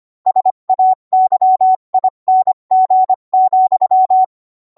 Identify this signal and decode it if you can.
Morse code, heard as a keyed tone, about 25 wpm, 750 Hz. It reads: SAYING,